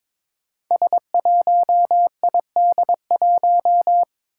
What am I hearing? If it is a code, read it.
Morse code, heard as a keyed tone, about 21 wpm, 700 Hz. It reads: S1ID1
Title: Five Challenging Letter-Number Combinations 22wpm